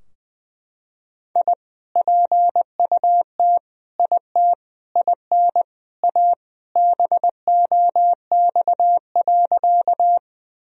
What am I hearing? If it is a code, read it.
Morse code, heard as a keyed tone, about 20 wpm, 700 Hz. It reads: I PUT IT IN A BOX.
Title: Sentences from Top 1000 Words - Review Part 2 - 20wpm